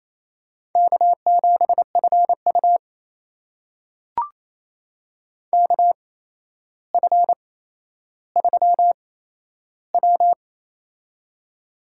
Morse code, heard as a keyed tone, about 28 wpm, spaced as for 12 wpm, 700 Hz. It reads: K7FU KF3W
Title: USA Call Signs - 4 Characters - 8x Character Spacing - 28wpm